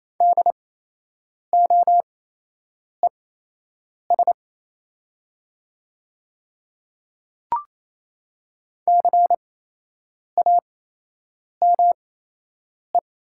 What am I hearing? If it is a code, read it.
Morse code, heard as a keyed tone, about 28 wpm, spaced as for 8 wpm, 700 Hz. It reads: DOES CAME
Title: 8x Character Spacing - Top 500 Words - 28wpm